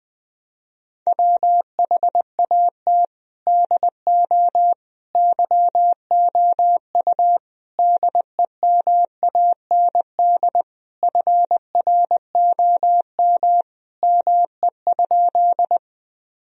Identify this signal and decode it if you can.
Morse code, heard as a keyed tone, about 20 wpm, 700 Hz. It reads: WHAT DO YOU DEMAND FROM ME?
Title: Sentences from Top 1000 Words - Review Part 4 - 20wpm